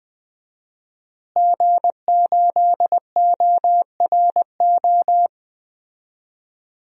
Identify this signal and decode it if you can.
Morse code, heard as a keyed tone, about 20 wpm, 700 Hz. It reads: G8ORO
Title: DXCC-100 Call Signs - 5 Characters - Repeated 3x - 20wpm